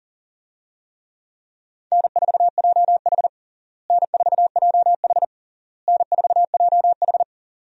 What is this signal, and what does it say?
Morse code, heard as a keyed tone, about 40 wpm, 700 Hz. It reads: N4JH N4JH N4JH
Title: USA Call Signs - 4 Characters - Repeated 3x - 40wpm